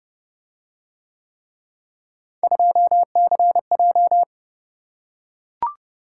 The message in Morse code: 2CJ

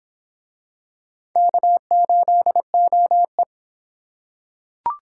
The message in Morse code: K8OE

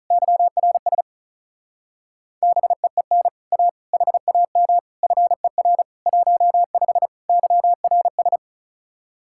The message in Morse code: YRS BEEN A HAM FER 15 YRS

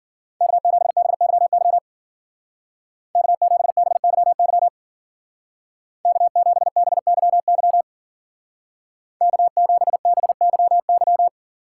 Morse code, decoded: K7BYY K7BYY K7BYY K7BYY

